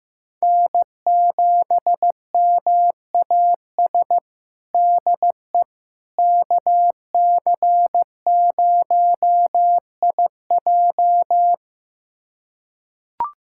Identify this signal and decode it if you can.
Morse code, heard as a keyed tone, about 15 wpm, 700 Hz. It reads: N7MAS DE KC0IJ
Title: USA Callsign DE Callsign - 5 Characters - 15wpm